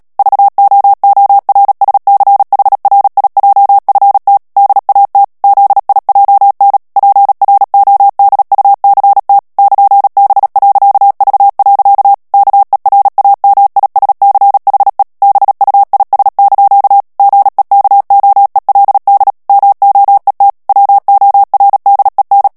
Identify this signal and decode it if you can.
Morse code, heard as a keyed tone, about 37 wpm, 800 Hz. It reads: VOORSCHRIJFT DAT ZIJN PRODUCT <KN>B.V. KERAMISCHE BUIS) GEKOELD MOET WORDEN
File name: morse mp3 / HOLLANDAIS texte clair 2 vitesse 35 mots minute